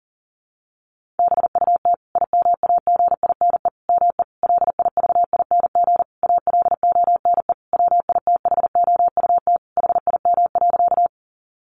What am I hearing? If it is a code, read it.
Morse code, heard as a keyed tone, about 37 wpm, 700 Hz. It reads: BUT IMAGINE ME LIVING ALONE WITHOUT HIM.
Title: Sentences from Top 1400 Words 40wpm